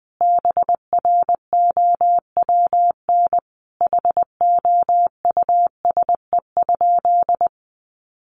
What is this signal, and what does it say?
Morse code, heard as a keyed tone, about 19 wpm, 700 Hz. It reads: BROWN HOUSE?
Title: Sentences from Top 1200 Words 20wpm